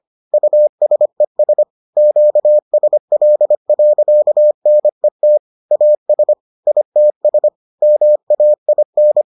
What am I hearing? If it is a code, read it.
Morse code, heard as a keyed tone, about 25 wpm, 600 Hz. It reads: USES QSL.NET AS ITS MAIN